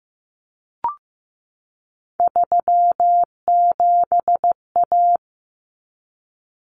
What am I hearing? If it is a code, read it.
Morse code, heard as a keyed tone, about 15 wpm, 700 Hz. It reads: 37A
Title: Common Cut Numbers - 3-Digit 15wpm